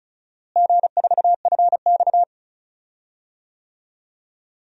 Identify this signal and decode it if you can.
Morse code, heard as a keyed tone, about 34 wpm, 700 Hz. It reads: G4FX
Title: DXCC-100 Call Signs - 4 Characters - Repeated 3x - 35wpm